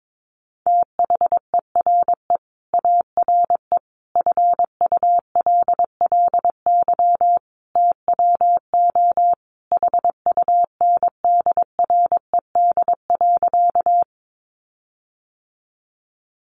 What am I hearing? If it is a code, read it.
Morse code, heard as a keyed tone, about 22 wpm, 700 Hz. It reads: THERE ARE FULLY TWO HUNDRED.
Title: Sentences from Top 1300 Words 22wpm